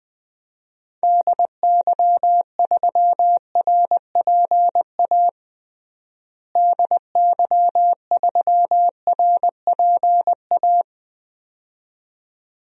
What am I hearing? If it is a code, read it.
Morse code, heard as a keyed tone, about 20 wpm, 700 Hz. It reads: DY3RPA DY3RPA
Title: DXCC-100 Call Signs - 6 Characters - Repeated 2x - 20wpm